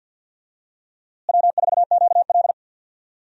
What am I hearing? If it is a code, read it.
Morse code, heard as a keyed tone, about 52 wpm, 700 Hz. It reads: W4QL
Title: USA Call Signs - 4 Characters - Repeated 3x - 50wpm